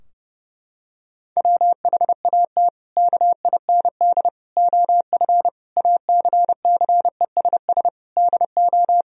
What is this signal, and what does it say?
Morse code, heard as a keyed tone, about 30 wpm, 700 Hz. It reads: WHAT KIND OF ACCESS DO
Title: Sentences from Top 1000 Words - Review Part 4 - 30wpm